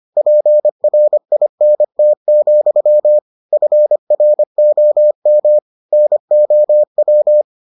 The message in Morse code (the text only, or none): PRINT, FROM NOW